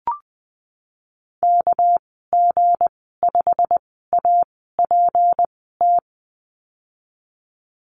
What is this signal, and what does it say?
Morse code, heard as a keyed tone, about 19 wpm, spaced as for 14 wpm, 700 Hz. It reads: KG5APT
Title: 213 Lesson 34g - Number 8 - Call Signs 20wpm